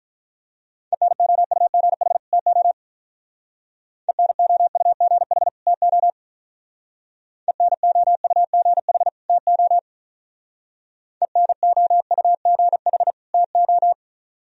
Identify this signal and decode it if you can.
Morse code, heard as a keyed tone, about 43 wpm, 700 Hz. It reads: ENOUGH TO ENOUGH TO ENOUGH TO ENOUGH TO
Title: Speed-Racing - Sets of 2 Words 35wpm